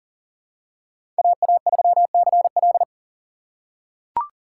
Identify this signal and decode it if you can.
Morse code, heard as a keyed tone, about 38 wpm, 700 Hz. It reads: AA3CL
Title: USA Call Signs - 4x Character Spacing - 40wpm